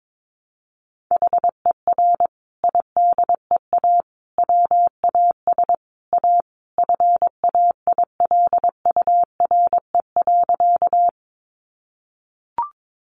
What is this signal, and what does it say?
Morse code, heard as a keyed tone, about 22 wpm, 700 Hz. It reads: HER IDEA WAS A FAILURE.
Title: Sentences from Top 900 Words 22wpm